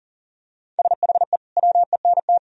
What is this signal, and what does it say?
Morse code, heard as a keyed tone, about 37 wpm, 700 Hz. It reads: SHE WENT
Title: Sentences from Top 1000 Words - Review Part 2 - 40wpm